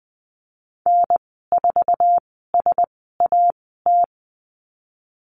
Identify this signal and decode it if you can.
Morse code, heard as a keyed tone, about 20 wpm, spaced as for 15 wpm, 700 Hz. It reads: N4SAT
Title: 069 Lesson 12g - Letter d - Call Signs 20wpm